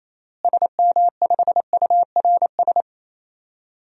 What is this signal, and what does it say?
Morse code, heard as a keyed tone, about 28 wpm, 700 Hz. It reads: SM5URS
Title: DXCC-100 Call Signs - 6 Characters - Repeated 3x - 28wpm